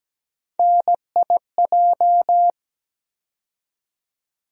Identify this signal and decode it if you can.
Morse code, heard as a keyed tone, about 17 wpm, 700 Hz. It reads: NIJ
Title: USA Call Sign Postfixes - 3 Letters 17wpm